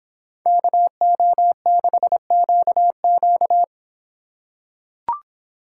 Morse code, heard as a keyed tone, about 24 wpm, 700 Hz. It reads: KO6QQ